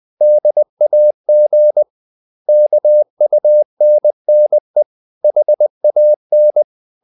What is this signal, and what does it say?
Morse code, heard as a keyed tone, about 20 wpm, 600 Hz. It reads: DAG KUNNE HAN
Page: Ferie 20 wpm